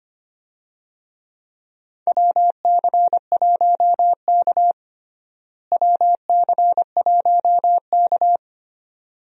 Morse code, heard as a keyed tone, about 25 wpm, 700 Hz. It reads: WC1K WC1K
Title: USA Call Signs - 4 Characters - Repeated 2x - 25wpm